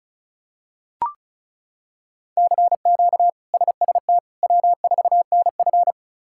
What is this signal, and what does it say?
Morse code, heard as a keyed tone, about 36 wpm, 700 Hz. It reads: CQ SST W4NF
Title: K1USN SST QSO Practice - 35wpm